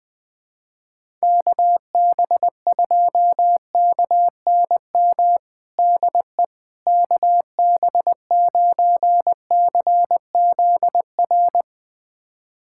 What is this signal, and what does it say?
Morse code, heard as a keyed tone, about 20 wpm, 700 Hz. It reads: KB2KNM DE KB9CZR
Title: USA Callsign DE Callsign - 6 Characters - 20wpm